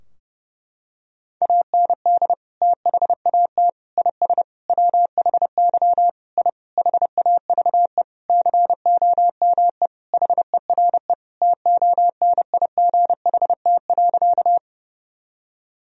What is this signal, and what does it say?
Morse code, heard as a keyed tone, about 30 wpm, 700 Hz. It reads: AND THAT IS WHY I HAVE COME HERE TONIGHT.
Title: Sentences from Top 1000 Words - Review Part 3 - 30wpm